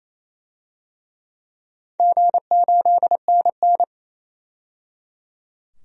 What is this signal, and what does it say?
Morse code, heard as a keyed tone, about 28 wpm, 700 Hz. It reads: G8NN
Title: DXCC-100 Call Signs - 4 Characters - 28wpm